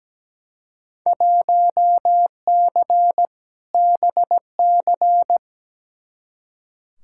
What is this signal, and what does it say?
Morse code, heard as a keyed tone, about 17 wpm, 700 Hz. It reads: 1C BC
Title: Field Day Exchanges Encore - 17wpm